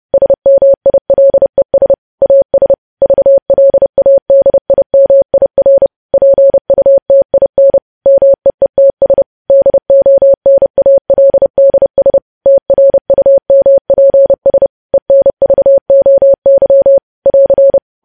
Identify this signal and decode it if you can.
Morse code, heard as a keyed tone, about 30 wpm, 550 Hz. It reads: SMILES AS VLADIMIR PUTIN MEETS DONALDS TRUMPS ENVOY <AR>
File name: podcast / News headlines-1x-30x30